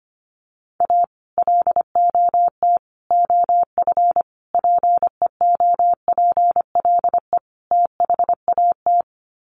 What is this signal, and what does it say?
Morse code, heard as a keyed tone, about 25 wpm, 700 Hz. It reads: A LOT OF PEOPLE THAT
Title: Sets of 5 Words Encore2 25wpm